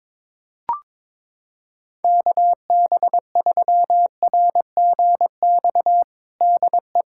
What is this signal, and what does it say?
Morse code, heard as a keyed tone, about 22 wpm, 700 Hz. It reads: KB3RGX DE
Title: Practice QSO - Exchange 4 - Ending - 22wpm